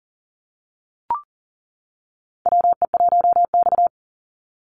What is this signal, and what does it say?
Morse code, heard as a keyed tone, about 40 wpm, 690 Hz. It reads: WE1X
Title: USA Call Signs 40wpm